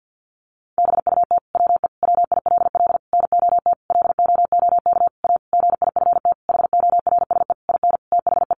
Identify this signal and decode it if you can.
Morse code, heard as a keyed tone, about 50 wpm, 700 Hz. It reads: BUT WE WILL NOT LOOK A GIFT HORSE IN THE
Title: Sentences from Top 1500 Words 50wpm